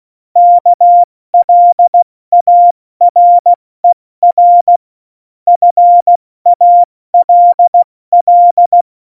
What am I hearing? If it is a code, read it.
Morse code, heard as a keyed tone, about 16 wpm, 700 Hz. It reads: KLARER FALL
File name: Morsekurs / Witze / Witz 09. Januar 2018, 70 F80